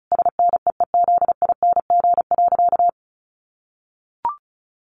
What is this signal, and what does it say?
Morse code, heard as a keyed tone, about 34 wpm, 700 Hz. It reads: SNEEZING.
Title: Easy Phrases 35wpm